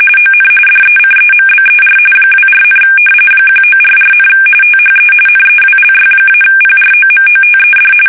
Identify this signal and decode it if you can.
RTTY, about 45.45 baud. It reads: ON DECEMBER 24, 1944, THE NEWLY-PROMOTED
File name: fsk850 / battleshipnj